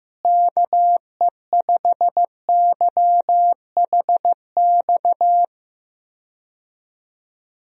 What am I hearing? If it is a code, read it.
Morse code, heard as a keyed tone, about 15 wpm, 700 Hz. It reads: KE5YHX
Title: USA Call Signs - 6 Characters - Repeated 3x - 15wpm